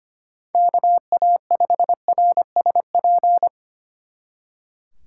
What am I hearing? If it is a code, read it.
Morse code, heard as a keyed tone, about 31 wpm, 700 Hz. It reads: KA5RSP